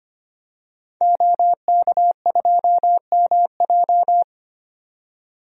Morse code, heard as a keyed tone, about 25 wpm, 700 Hz. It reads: OK2MJ
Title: DXCC-100 Call Signs - 5 Characters - 25wpm